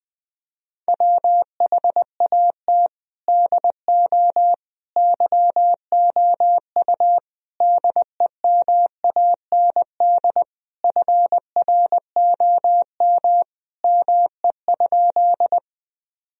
Morse code, heard as a keyed tone, about 20 wpm, 700 Hz. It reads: WHAT DO YOU DEMAND FROM ME?
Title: Sentences from Top 1000 Words - Review Part 4 - 20wpm